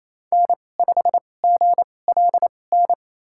Morse code, heard as a keyed tone, about 28 wpm, spaced as for 20 wpm, 700 Hz. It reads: N5GLN
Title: USA Call Signs - 5 Characters - 2x Character Spacing - 28wpm